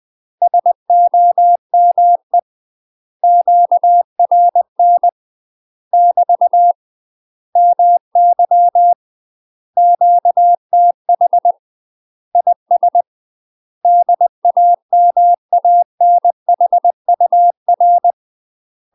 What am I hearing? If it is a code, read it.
Morse code, heard as a keyed tone, about 20 wpm, 700 Hz. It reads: SOME QRN <BT> MY QTH IS DAMANHUR
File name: UK MASTER QSOs TRAINING / 20WPM / UK MASTER QSOs 0023